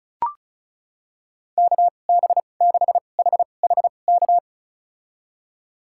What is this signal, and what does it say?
Morse code, heard as a keyed tone, about 36 wpm, spaced as for 25 wpm, 700 Hz. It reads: KB6HHK
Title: USA Call Signs - 2x Character Spacing - 35wpm